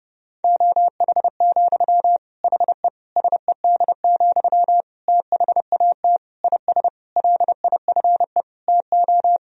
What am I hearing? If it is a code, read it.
Morse code, heard as a keyed tone, about 32 wpm, 700 Hz. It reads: OH, HE SED, THAT IS LIFE TO
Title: Sentences from Top 500 Words and Abbreviations 30wpm